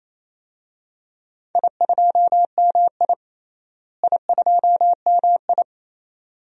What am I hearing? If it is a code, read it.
Morse code, heard as a keyed tone, about 28 wpm, 700 Hz. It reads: I2MI I2MI
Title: DXCC-100 Call Signs - 4 Characters - Repeated 2x - 28wpm